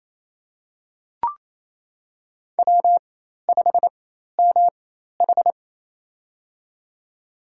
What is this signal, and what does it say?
Morse code, heard as a keyed tone, about 28 wpm, spaced as for 13 wpm, 700 Hz. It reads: W5MH
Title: USA Call Signs - 4 Characters - 4x Character Spacing - 28wpm